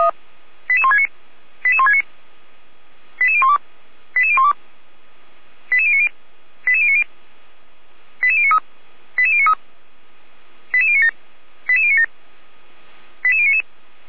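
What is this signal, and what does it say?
Five-tone selective call (ZVEI), heard as quick runs of five tones, tones of about 70 ms. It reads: 88169 88169 88012 88012 88090 88090 88093 88093 88097 88097 88099
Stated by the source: ZVEI five-tone call 88040 received via SDR.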